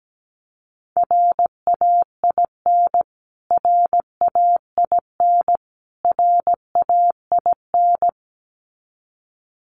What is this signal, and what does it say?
Morse code, heard as a keyed tone, about 17 wpm, 700 Hz. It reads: RAIN RAIN RAIN
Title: Top 500 Words Repeated 17wpm